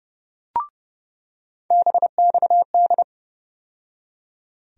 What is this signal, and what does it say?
Morse code, heard as a keyed tone, about 30 wpm, 700 Hz. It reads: BXD
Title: USA Call Sign Postfixes - 3 Letters 30wpm